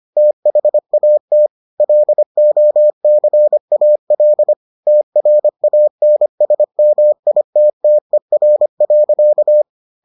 Morse code, heard as a keyed tone, about 25 wpm, 600 Hz. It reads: THAT LOCAL TRANSMITTER.